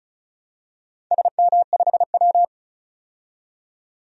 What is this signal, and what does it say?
Morse code, heard as a keyed tone, about 36 wpm, 700 Hz. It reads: SM5W
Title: DXCC-100 Call Signs - 4 Characters - 35wpm